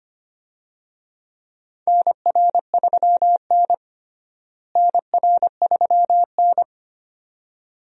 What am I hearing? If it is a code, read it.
Morse code, heard as a keyed tone, about 25 wpm, 700 Hz. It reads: NR3N NR3N
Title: USA Call Signs - 4 Characters - Repeated 2x - 25wpm